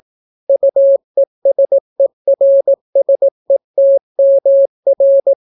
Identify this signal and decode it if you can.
Morse code, heard as a keyed tone, about 18 wpm, 550 Hz. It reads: UESERSETMR